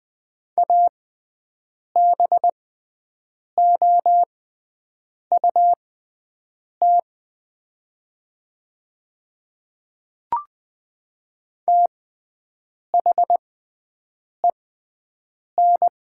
Morse code, heard as a keyed tone, about 20 wpm, spaced as for 7 wpm, 700 Hz. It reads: ABOUT THEN
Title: 6x Character Spacing - Top 100 Words - 20wpm